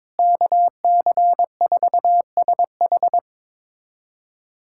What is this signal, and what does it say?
Morse code, heard as a keyed tone, about 22 wpm, 700 Hz. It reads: KC4SH
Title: USA Callsign DE Callsign - 5 Characters - 22wpm